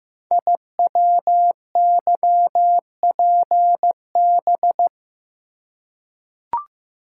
Morse code, heard as a keyed tone, about 15 wpm, 700 Hz. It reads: IWYPB
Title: Code Groups - 5 Letters - 15wpm